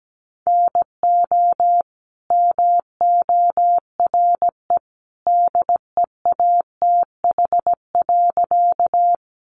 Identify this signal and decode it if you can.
Morse code, heard as a keyed tone, about 17 wpm, 700 Hz. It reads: NO MORE DEATH.
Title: Sentences from Top 1100 Words 17wpm